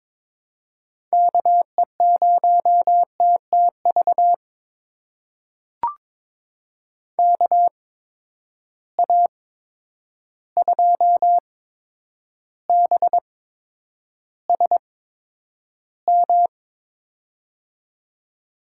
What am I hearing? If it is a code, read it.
Morse code, heard as a keyed tone, about 22 wpm, spaced as for 9 wpm, 700 Hz. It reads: KE0TTV KA2BSM
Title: USA Call Signs - 6 Characters - 8x Character Spacing - 22wpm